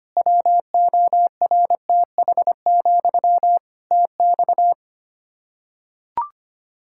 Morse code, heard as a keyed tone, about 25 wpm, 700 Hz. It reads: WORTH, TX
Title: Practice QSO - Exchange 1 - Beginning - 25wpm